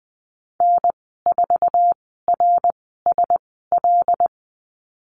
Morse code, heard as a keyed tone, about 20 wpm, spaced as for 14 wpm, 700 Hz. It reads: N4RSL
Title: 082 Lesson 14g - Number 2 - Call Signs 20wpm